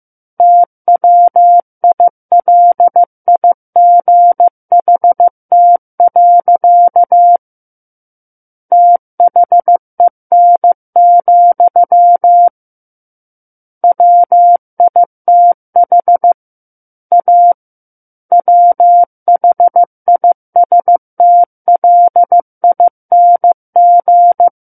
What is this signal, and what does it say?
Morse code, heard as a keyed tone, about 15 wpm, 700 Hz. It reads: TWILIGHT. THEN, WITH A WHISTLING
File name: Chapter 6 - 15 WPM